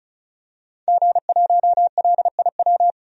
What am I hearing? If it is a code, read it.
Morse code, heard as a keyed tone, about 36 wpm, 700 Hz. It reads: G1LIW
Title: DXCC-100 Call Signs - 5 Characters - 35wpm